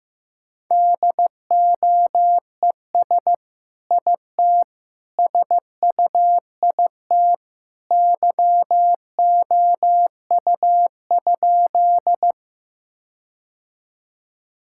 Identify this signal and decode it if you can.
Morse code, heard as a keyed tone, about 15 wpm, 700 Hz. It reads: DOES IT SUIT YOU?
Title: Sentences from Top 1600 Words 15wpm